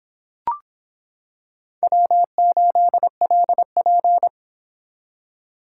Morse code, heard as a keyed tone, about 26 wpm, 700 Hz. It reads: W8LP